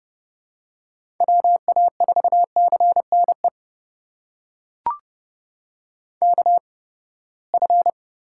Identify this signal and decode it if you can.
Morse code, heard as a keyed tone, about 30 wpm, spaced as for 21 wpm, 700 Hz. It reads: WA4CNE KF
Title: USA Call Signs - 6 Characters - 8x Character Spacing - 30wpm